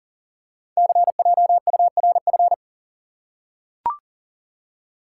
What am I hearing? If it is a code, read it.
Morse code, heard as a keyed tone, about 43 wpm, 700 Hz. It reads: CJURF
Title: Code Groups - 5 Letters - 40wpm